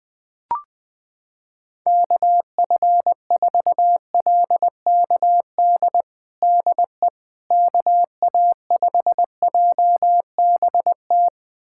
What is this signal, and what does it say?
Morse code, heard as a keyed tone, about 20 wpm, 700 Hz. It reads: KF4LKD DE KA5JBT
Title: USA Callsign DE Callsign - 6 Characters - 20wpm